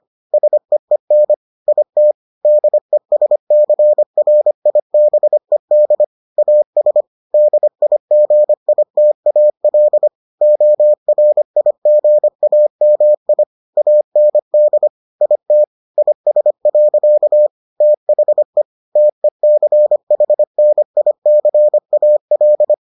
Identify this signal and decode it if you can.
Morse code, heard as a keyed tone, about 25 wpm, 600 Hz. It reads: SEEN IT DESCRIBED AS DIGITAL ORIGAMI AND IT IS. THE TECHNICAL